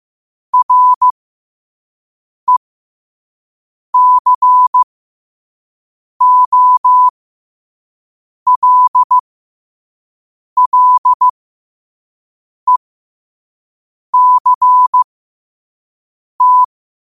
Morse code, heard as a keyed tone, about 15 wpm, spaced as for 5 wpm, 1000 Hz. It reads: RECOLLECT
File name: qotdfiles / 5wpm